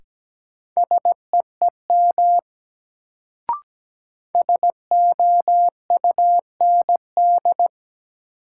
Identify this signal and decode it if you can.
Morse code, heard as a keyed tone, about 17 wpm, 700 Hz. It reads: SEEM SOUND
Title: Sets of 3 Words - New Word Drill - 17wpm